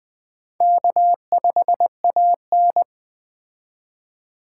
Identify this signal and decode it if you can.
Morse code, heard as a keyed tone, about 20 wpm, 700 Hz. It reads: K5AN
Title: Speed Racing - USA 1x2 Call Signs 17wpm